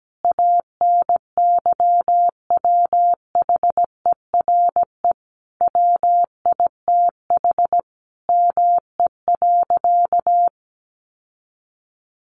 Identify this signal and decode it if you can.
Morse code, heard as a keyed tone, about 17 wpm, 700 Hz. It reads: ANYWHERE WITH ME.
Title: Sentences from Top 1400 Words 17wpm